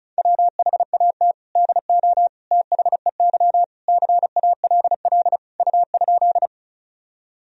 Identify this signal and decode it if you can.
Morse code, heard as a keyed tone, about 36 wpm, 700 Hz. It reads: WHAT DO THEY CALL U?
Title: Sentences from Top 100 Words and Abbreviations 35wpm